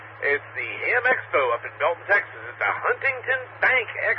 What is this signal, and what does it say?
Emergency Alert System (SAME) tones, heard as harsh buzzing bursts, with speech. It is heard from 0.8 to 4.0 s.